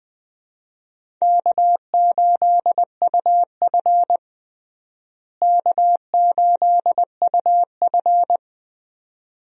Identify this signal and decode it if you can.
Morse code, heard as a keyed tone, about 20 wpm, 700 Hz. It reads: K8UF K8UF
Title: USA Call Signs - 4 Characters - Repeated 2x - 20wpm